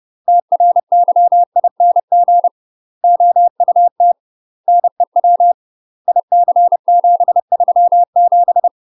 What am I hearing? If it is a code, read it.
Morse code, heard as a keyed tone, about 30 wpm, 700 Hz. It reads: TRYING OUT NEW IC737
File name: UK MASTER QSOs TRAINING / 30WPM / UK MASTER QSOs 0033